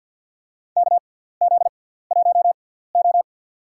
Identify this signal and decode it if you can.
Morse code, heard as a keyed tone, about 47 wpm, spaced as for 17 wpm, 700 Hz. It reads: KZ1O
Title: USA Call Signs - 4 Characters - 6x Character Spacing - 50wpm